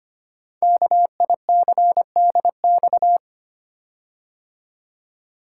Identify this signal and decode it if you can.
Morse code, heard as a keyed tone, about 25 wpm, 700 Hz. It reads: KICDX
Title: Code Groups - 5 Letter:Numbers - 25wpm